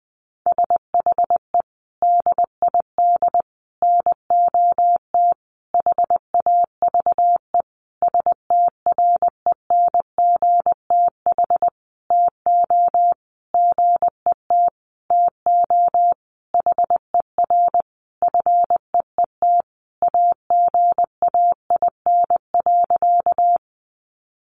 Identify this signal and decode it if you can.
Morse code, heard as a keyed tone, about 20 wpm, 700 Hz. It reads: SHE DID NOT HAVE STRENGTH TO GET TO HER FEET AGAIN.
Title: Sentences from Top 1300 Words 20wpm